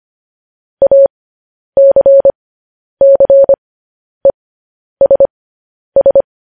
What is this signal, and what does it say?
Morse code, heard as a keyed tone, about 25 wpm, spaced as for 10 wpm, 550 Hz. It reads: ACCESS
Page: news-headlines-in-morse-code-at-10-wpm